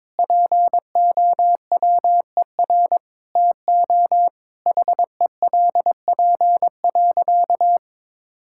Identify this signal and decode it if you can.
Morse code, heard as a keyed tone, about 22 wpm, 700 Hz. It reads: POWER TO HELP.
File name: practice / 22wpm / Sentences with 8-Word Phrases - 22wpm